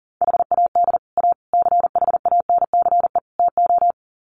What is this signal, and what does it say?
Morse code, heard as a keyed tone, about 40 wpm, 700 Hz. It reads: HAD A CHANCE TO
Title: Sets of 4 Words 40wpm